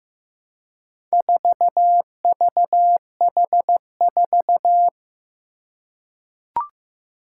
Morse code, heard as a keyed tone, about 15 wpm, 700 Hz. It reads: 4VH4
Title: Four Challenging Letter-Number Combinations 15wpm